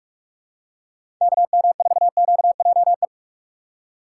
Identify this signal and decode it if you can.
Morse code, heard as a keyed tone, about 46 wpm, 700 Hz. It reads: KM4QJE
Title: USA Call Signs - 6 Characters - 2x Character Spacing - 45wpm